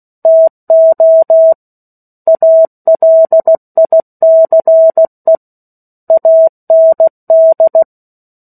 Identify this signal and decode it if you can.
Morse code, heard as a keyed tone, about 16 wpm, 650 Hz. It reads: TO ALICE AND
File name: 03-Chapter-03 16 WPM